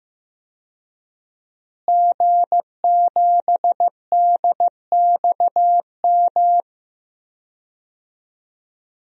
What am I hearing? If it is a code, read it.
Morse code, heard as a keyed tone, about 15 wpm, 700 Hz. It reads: G7DXM
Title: DXCC-100 Call Signs - 5 Characters - Repeated 2x - 15wpm